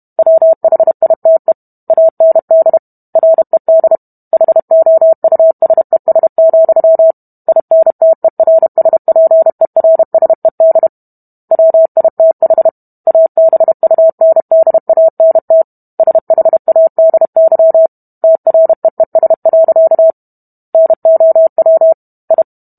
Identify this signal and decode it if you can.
Morse code, heard as a keyed tone, about 32 wpm, 650 Hz. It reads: WHITE AND RED HOUSES, INTERSPERSED WITH ABUNDANT SHADY TREES. NOW I